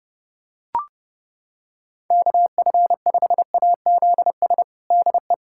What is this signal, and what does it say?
Morse code, heard as a keyed tone, about 30 wpm, 700 Hz. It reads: KF5AZS DE
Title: USA Callsign DE Callsign - 6 Characters - 30wpm